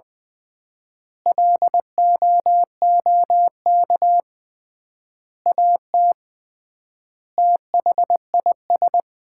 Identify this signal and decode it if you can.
Morse code, heard as a keyed tone, about 20 wpm, 700 Hz. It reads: LOOK AT THIS
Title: 3x Word-Spacing - Sets of 5 Words - 20wpm